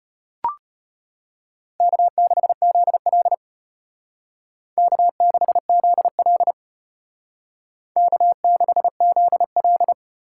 Morse code, heard as a keyed tone, about 32 wpm, 700 Hz. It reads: K6ZL K6ZL K6ZL